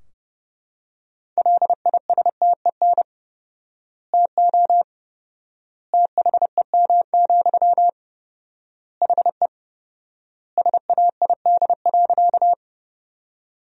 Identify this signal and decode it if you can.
Morse code, heard as a keyed tone, about 30 wpm, 700 Hz. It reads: LISTEN TO THEM, HE SAID.
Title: 4x Word-Spacing - Sentences from Top 400 Words - 30wpm